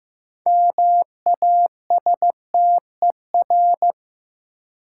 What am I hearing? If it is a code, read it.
Morse code, heard as a keyed tone, about 15 wpm, 700 Hz. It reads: MASTER
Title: Sentences from Top 1300 Words - New Word Drill - 15wpm